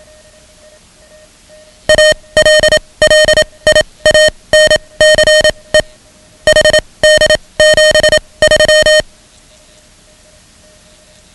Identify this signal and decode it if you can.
Morse code, heard as a keyed tone, about 27 wpm, spaced as for 21 wpm, 620 Hz. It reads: ALLIANCE HD73
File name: ham radio / CodeQuick / Code Quick 10